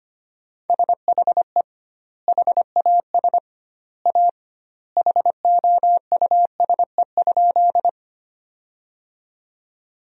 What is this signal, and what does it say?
Morse code, heard as a keyed tone, about 25 wpm, 700 Hz. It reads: SHE HAS A HOUSE?
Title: 2x Word-Spacing - Sentences from Top 200 Words - 25wpm